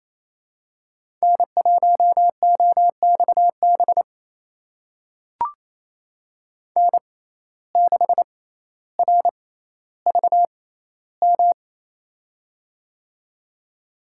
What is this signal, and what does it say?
Morse code, heard as a keyed tone, about 27 wpm, spaced as for 14 wpm, 700 Hz. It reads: N1OXB N6RVM
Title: USA Call Signs - 5 Characters - 6x Character Spacing - 28wpm